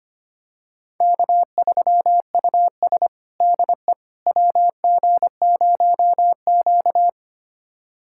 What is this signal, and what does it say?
Morse code, heard as a keyed tone, about 25 wpm, 700 Hz. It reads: K3US DE WG0Q
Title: USA Callsign DE Callsign - 4 Characters - 25wpm